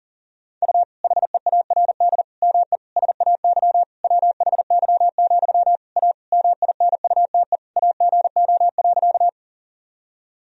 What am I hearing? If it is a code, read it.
Morse code, heard as a keyed tone, about 40 wpm, 700 Hz. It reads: U HEARD ME SAY WHY, A MINUTE AGO.
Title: Sentences from Top 500 Words and Abbreviations 40wpm